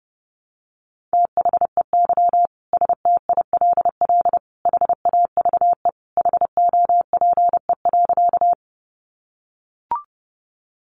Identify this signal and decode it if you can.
Morse code, heard as a keyed tone, about 30 wpm, 700 Hz. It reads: THEY STILL HAVE HOPE.
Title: Sentences from Top 1000 Words 30wpm